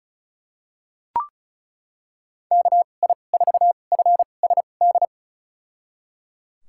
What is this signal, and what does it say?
Morse code, heard as a keyed tone, about 36 wpm, spaced as for 25 wpm, 700 Hz. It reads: KI4FSD
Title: USA Call Signs - 6 Characters - 2x Character Spacing - 35wpm